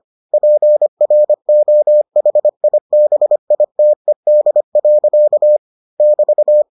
Morse code, heard as a keyed tone, about 25 wpm, 600 Hz. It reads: PROHIBITED. <BT>